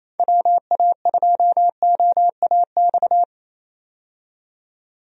Morse code, heard as a keyed tone, about 28 wpm, 700 Hz. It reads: WA2OAX
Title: USA Call Signs - 6 Characters - 28wpm